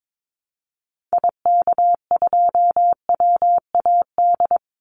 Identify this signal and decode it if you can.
Morse code, heard as a keyed tone, about 22 wpm, 700 Hz. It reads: IK2WAD
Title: Call Signs Repeated 3x - 22wpm